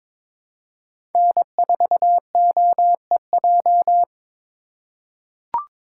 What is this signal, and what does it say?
Morse code, heard as a keyed tone, about 22 wpm, 700 Hz. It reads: N4OEJ
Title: USA Call Signs - 5 Characters - 3x Character Spacing - 22wpm